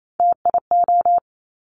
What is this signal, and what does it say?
Morse code, heard as a keyed tone, about 28 wpm, 700 Hz. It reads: TIO
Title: Top Three Letter Combinations in Words 28wpm